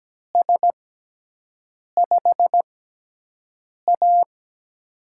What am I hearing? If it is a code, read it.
Morse code, heard as a keyed tone, about 17 wpm, spaced as for 6 wpm, 700 Hz. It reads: S5A
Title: USA Call Signs - 4 Characters - 6x Character Spacing - 17wpm